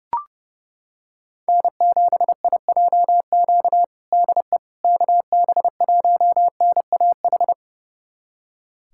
Morse code, heard as a keyed tone, about 30 wpm, 700 Hz. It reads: N7IJQ DE KB1NAH
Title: USA Callsign DE Callsign 30wpm